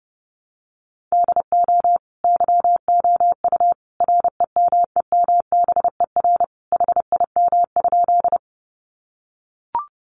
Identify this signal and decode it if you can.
Morse code, heard as a keyed tone, about 30 wpm, 700 Hz. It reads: DO YOU REMEMBER HIM?
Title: Sentences from Top 400 Words 30wpm